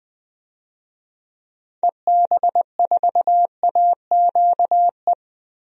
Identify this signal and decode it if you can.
Morse code, heard as a keyed tone, about 20 wpm, 700 Hz. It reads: EB4AQE
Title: DXCC-100 Call Signs - 6 Characters - Repeated 2x - 20wpm